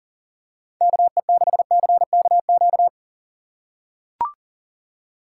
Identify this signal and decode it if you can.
Morse code, heard as a keyed tone, about 40 wpm, 700 Hz. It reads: KE6CKQ
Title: USA Call Signs - 6 Characters - 4x Character Spacing - 40wpm